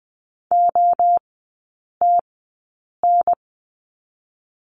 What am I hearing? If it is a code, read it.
Morse code, heard as a keyed tone, about 20 wpm, 700 Hz. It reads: O T N
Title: 023 Lesson 5e - Letter o - Three Characters 20wpm